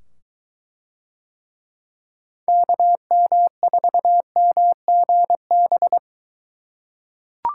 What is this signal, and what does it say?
Morse code, heard as a keyed tone, about 23 wpm, 700 Hz. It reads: KM4MGB